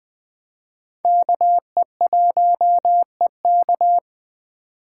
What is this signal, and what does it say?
Morse code, heard as a keyed tone, about 20 wpm, 700 Hz. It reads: KE1EK
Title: USA Call Signs - 5 Characters - 2x Character Spacing - 20wpm